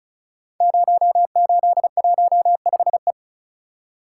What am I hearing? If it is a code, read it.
Morse code, heard as a keyed tone, about 36 wpm, 700 Hz. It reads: 0815E
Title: Code Groups - 5 Letter:Numbers - 35wpm